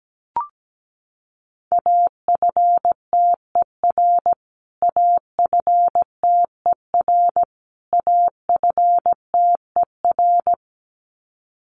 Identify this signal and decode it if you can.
Morse code, heard as a keyed tone, about 17 wpm, 700 Hz. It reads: AFTER AFTER AFTER
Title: Top 300 Words Repeated 17wpm